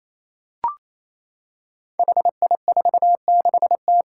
Morse code, heard as a keyed tone, about 27 wpm, 700 Hz. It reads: HI46T
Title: Five Challenging Letter-Number Combinations 28wpm